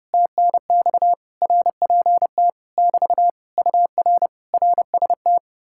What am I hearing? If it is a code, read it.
Morse code, heard as a keyed tone, about 33 wpm, 700 Hz. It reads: TNX RPT <BT> UR RST